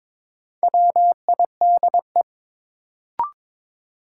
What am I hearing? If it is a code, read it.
Morse code, heard as a keyed tone, about 21 wpm, 700 Hz. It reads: WIDE
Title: Sentences from Top 900 Words - New Word Drill - 22wpm